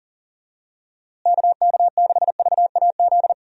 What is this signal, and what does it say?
Morse code, heard as a keyed tone, about 40 wpm, 700 Hz. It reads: KK6VAZ
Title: USA Call Signs - 6 Characters - Repeated 3x - 40wpm